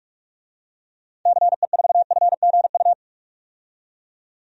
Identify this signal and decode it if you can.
Morse code, heard as a keyed tone, about 46 wpm, 700 Hz. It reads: CE4FGU
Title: DXCC-100 Call Signs - 6 Characters - 45wpm